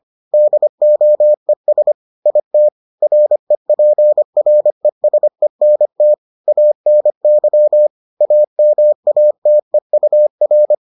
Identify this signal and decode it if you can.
Morse code, heard as a keyed tone, about 25 wpm, 600 Hz. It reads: DOES IT REPRESENT ANY AMATEUR